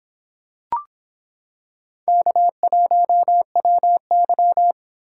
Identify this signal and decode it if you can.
Morse code, heard as a keyed tone, about 25 wpm, 700 Hz. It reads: K1WY